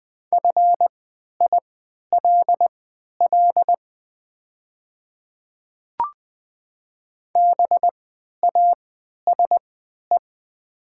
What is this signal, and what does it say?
Morse code, heard as a keyed tone, about 20 wpm, spaced as for 11 wpm, 700 Hz. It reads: FILL BASE
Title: 3x Character Spacing - Top 500 Words - 20wpm